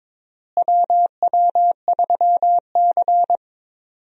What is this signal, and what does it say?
Morse code, heard as a keyed tone, about 22 wpm, 700 Hz. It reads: WW3C
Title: USA Call Signs - 4 Characters - Repeated 3x - 22wpm